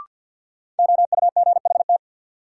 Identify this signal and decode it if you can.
Morse code, heard as a keyed tone, about 51 wpm, 700 Hz. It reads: OUGHT
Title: Sets of 5 Words Encore - New Word Drill - 50wpm